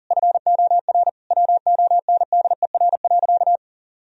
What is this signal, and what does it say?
Morse code, heard as a keyed tone, about 40 wpm, 700 Hz. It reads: FOR WONDER.
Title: Sentences from Top 1000 Words - Review Part 3 - 40wpm